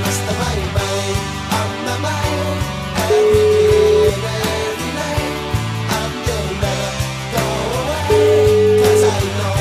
Ringback tone (Europe, 425 Hz), with noise.